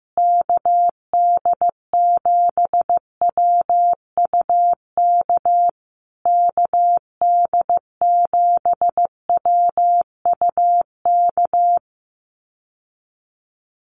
Morse code, heard as a keyed tone, about 15 wpm, 700 Hz. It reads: KD7WUK KD7WUK
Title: USA Call Signs Repeated 3x 15wpm